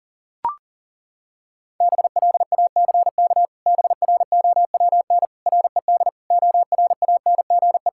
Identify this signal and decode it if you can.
Morse code, heard as a keyed tone, about 37 wpm, 700 Hz. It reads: BLACK BROWN RED ORANGE
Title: Pangram Challenge v2 - 40wpm